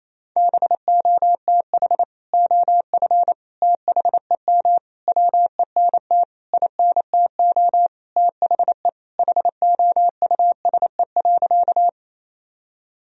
Morse code, heard as a keyed tone, about 28 wpm, 700 Hz. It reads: BOTH OF THEM WENT INTO THE HOUSE.
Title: Sentences from Top 1000 Words - Review Part 1 - 28wpm